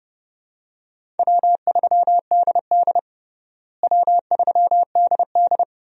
Morse code, heard as a keyed tone, about 30 wpm, 700 Hz. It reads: W3DD W3DD
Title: USA Call Signs - 4 Characters - Repeated 2x - 30wpm